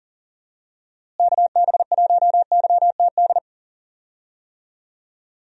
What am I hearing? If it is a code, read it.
Morse code, heard as a keyed tone, about 39 wpm, 700 Hz. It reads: KB1YTD
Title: USA Call Signs - 6 Characters - Repeated 2x - 40wpm